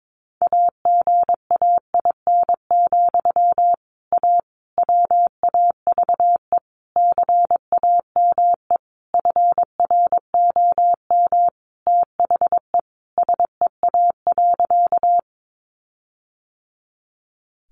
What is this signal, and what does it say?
Morse code, heard as a keyed tone, about 22 wpm, 700 Hz. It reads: AGAIN, A WAVE CAME FROM THE SEA.
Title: Sentences from Top 1400 Words 22wpm